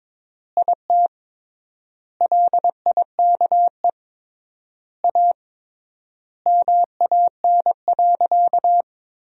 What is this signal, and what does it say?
Morse code, heard as a keyed tone, about 22 wpm, 700 Hz. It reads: IT LIKE A MAN.
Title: 3x Word-Spacing - Sentences from Top 300 Words - 22wpm